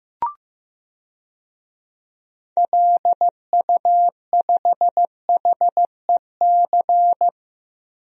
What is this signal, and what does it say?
Morse code, heard as a keyed tone, about 15 wpm, 700 Hz. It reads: LU5HEC
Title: DXCC-100 Call Signs - 6 Characters - Repeated 2x - 15wpm